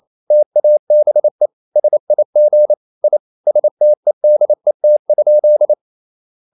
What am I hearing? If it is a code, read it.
Morse code, heard as a keyed tone, about 28 wpm, 600 Hz. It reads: TABE SIG I STEDET?
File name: Ferie QRQ 28wpm